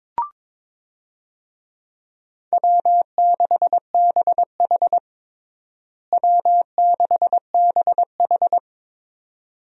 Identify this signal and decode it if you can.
Morse code, heard as a keyed tone, about 22 wpm, 700 Hz. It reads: W6BH W6BH
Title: USA Call Signs - 4 Characters - Repeated 2x - 22wpm